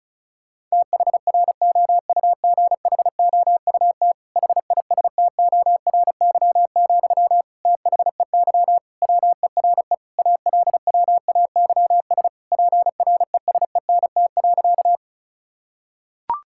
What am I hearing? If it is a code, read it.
Morse code, heard as a keyed tone, about 36 wpm, 700 Hz. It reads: THROUGHOUT HISTORY, THEY WERE ALWAYS PRESENT.
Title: Sentences from Top 1000 Words - Review Part 5 - 35wpm